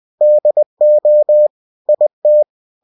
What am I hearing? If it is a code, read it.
Morse code, heard as a keyed tone, about 20 wpm, 600 Hz. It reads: DO IT
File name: cw-books / winnie-the-pooh / winnie-20wpm-page-0024